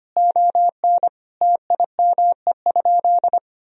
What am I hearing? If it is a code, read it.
Morse code, heard as a keyed tone, about 25 wpm, 700 Hz. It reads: ON TIME?
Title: Sentences from Top 1000 Words - Review Part 3 - 25wpm